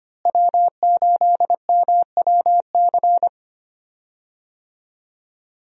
Morse code, heard as a keyed tone, about 25 wpm, 700 Hz. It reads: W8MWC
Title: USA Call Signs - 5 Characters - Repeated 2x - 25wpm